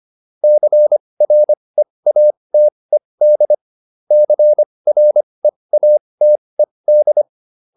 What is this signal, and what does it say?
Morse code, heard as a keyed tone, about 25 wpm, spaced as for 20 wpm, 600 Hz. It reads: CREATED CREATED